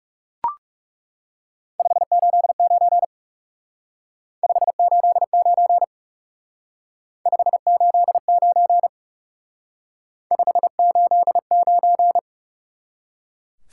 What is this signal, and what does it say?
Morse code, heard as a keyed tone, about 38 wpm, 700 Hz. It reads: 589 589 589 589